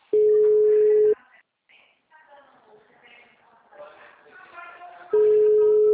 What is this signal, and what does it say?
Ringback tone (Europe, 425 Hz), with speech.
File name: out- -2051- - - .72508